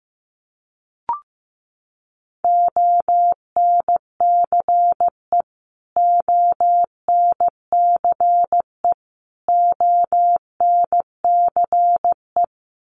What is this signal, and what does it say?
Morse code, heard as a keyed tone, about 15 wpm, 700 Hz. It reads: ONCE ONCE ONCE
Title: Top 300 Words Repeated 15wpm